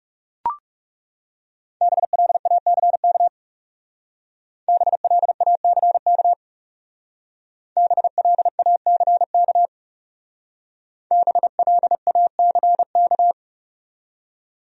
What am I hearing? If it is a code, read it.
Morse code, heard as a keyed tone, about 37 wpm, 700 Hz. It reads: BLACK BLACK BLACK BLACK
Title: Speed-Racing Top 500 Words 30wpm